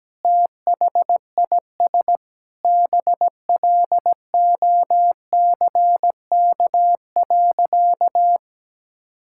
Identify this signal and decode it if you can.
Morse code, heard as a keyed tone, about 17 wpm, 700 Hz. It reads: THIS BLOCK.
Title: Sentences from Top 1600 Words 17wpm